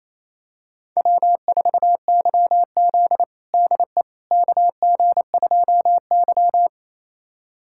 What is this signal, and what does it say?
Morse code, heard as a keyed tone, about 28 wpm, 700 Hz. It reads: W4YZ DE KG2Y
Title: USA Callsign DE Callsign - 4 Characters - 28wpm